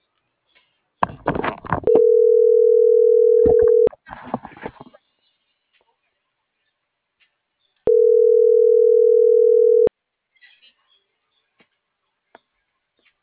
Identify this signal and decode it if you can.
Ringback tone (North America), 440 + 480 Hz.